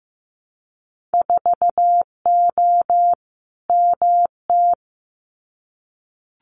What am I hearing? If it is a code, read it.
Morse code, heard as a keyed tone, about 15 wpm, 700 Hz. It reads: 4O MT
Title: Winter Field Day Exchanges Encore 15wpm